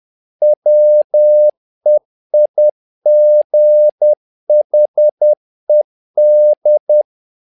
Morse code, heard as a keyed tone, about 10 wpm, 600 Hz. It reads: WEIGHED
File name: learning center / tower trouble / tower trouble 10wpm